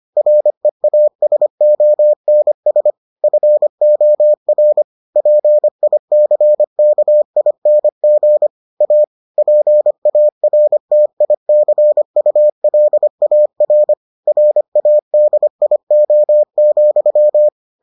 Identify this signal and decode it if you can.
Morse code, heard as a keyed tone, about 25 wpm, 600 Hz. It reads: REASONS FOR PICKING A PARTICULAR RADIO,